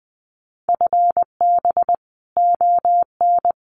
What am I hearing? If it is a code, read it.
Morse code, heard as a keyed tone, about 20 wpm, 700 Hz. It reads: FB ON
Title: Common QSO Phrases 20wpm